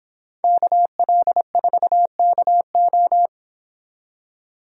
Morse code, heard as a keyed tone, about 26 wpm, 700 Hz. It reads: KL4KO